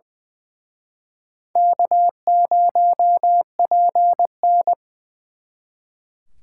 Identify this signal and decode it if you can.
Morse code, heard as a keyed tone, about 24 wpm, 700 Hz. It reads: K0PN
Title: Speed Racing - USA 1x2 Call Signs 20wpm